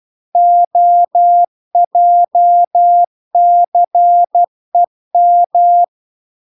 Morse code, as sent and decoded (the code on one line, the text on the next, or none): --- .--- -.-. --
OJCEM